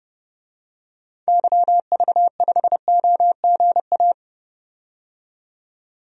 Morse code, heard as a keyed tone, about 30 wpm, 700 Hz. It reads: YV5OGA
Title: DXCC-100 Call Signs - 6 Characters - 30wpm